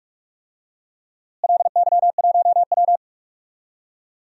Morse code, heard as a keyed tone, about 46 wpm, 700 Hz. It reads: LY1W
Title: DXCC-100 Call Signs - 4 Characters - Repeated 3x - 45wpm